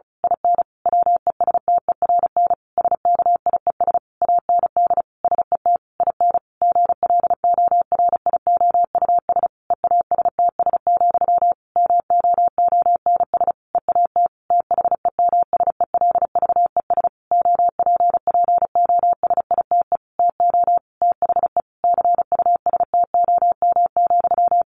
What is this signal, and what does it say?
Morse code, heard as a keyed tone, about 34 wpm, 700 Hz. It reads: IN WESTERN SKIES AND SET IN GLORIOUS EASTS, MOONS EAT THEMSELVES OPPOSITE TO THE CUSTOM,
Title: A Sound Of Thunder 35wpm